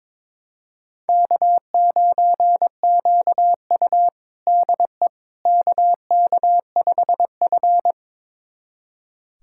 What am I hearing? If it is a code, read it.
Morse code, heard as a keyed tone, about 21 wpm, 700 Hz. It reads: K9QU DE KK5F
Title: USA Callsign DE Callsign - 4 Characters - 22wpm